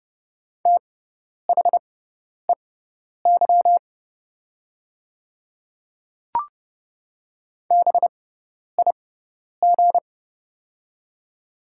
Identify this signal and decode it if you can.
Morse code, heard as a keyed tone, about 30 wpm, spaced as for 10 wpm, 700 Hz. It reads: THEY BIG
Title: 6x Character Spacing - Top 200 Words - 30wpm